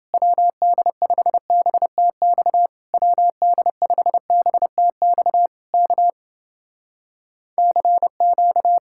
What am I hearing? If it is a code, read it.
Morse code, heard as a keyed tone, about 30 wpm, 700 Hz. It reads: WD5BTX WD5BTX K CQ